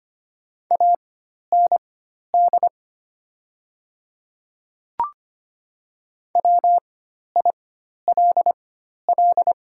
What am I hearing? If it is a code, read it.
Morse code, heard as a keyed tone, about 25 wpm, spaced as for 12 wpm, 700 Hz. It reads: AND WILL
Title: 4x Character Spacing - Top 100 Words - 25wpm